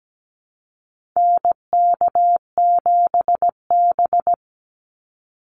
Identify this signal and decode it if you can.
Morse code, heard as a keyed tone, about 17 wpm, 700 Hz. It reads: NK7B
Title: Call Signs Repeated 2x - 17wpm